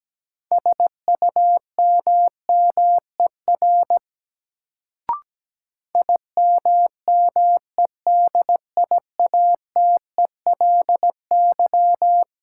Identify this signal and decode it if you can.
Morse code, heard as a keyed tone, about 17 wpm, 700 Hz. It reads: SUMMER IMMEDIATELY
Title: Sentences from Top 1100 Words - New Word Drill Alt - 17wpm